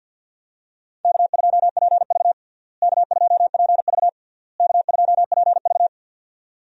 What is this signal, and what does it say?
Morse code, heard as a keyed tone, about 50 wpm, 700 Hz. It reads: K2PV K2PV K2PV
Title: USA Call Signs - 4 Characters - Repeated 3x - 50wpm